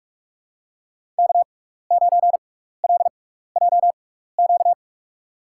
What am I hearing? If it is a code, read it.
Morse code, heard as a keyed tone, about 45 wpm, spaced as for 15 wpm, 700 Hz. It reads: K9LJQ
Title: USA Call Signs - 5 Characters - 6x Character Spacing - 45wpm